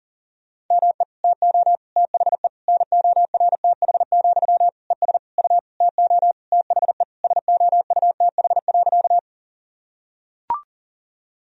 Morse code, heard as a keyed tone, about 40 wpm, 700 Hz. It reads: ME TO THE NORTH, ES U TO THE SOUTH.
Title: Sentences from Top 400 Words and Abbreviations 40wpm